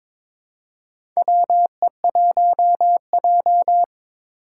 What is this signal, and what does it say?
Morse code, heard as a keyed tone, about 21 wpm, 700 Hz. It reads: WE1J
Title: USA Call Signs - 4 Characters - 3x Character Spacing - 22wpm